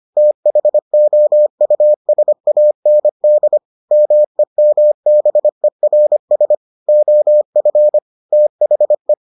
Morse code, heard as a keyed tone, about 25 wpm, 600 Hz. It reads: THOUSAND MEMBERS OF THE